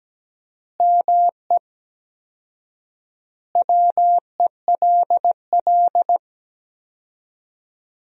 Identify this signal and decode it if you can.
Morse code, heard as a keyed tone, about 17 wpm, 700 Hz. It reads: ME WELL
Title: 4x Word-Spacing - Sentences from Top 400 Words - 17wpm